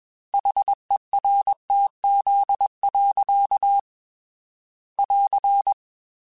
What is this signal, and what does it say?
Morse code, heard as a keyed tone, about 21 wpm, 800 Hz. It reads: HERTZ. <AR>
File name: Technical Subjects / Technical Subjects-mp3s / Reactance @20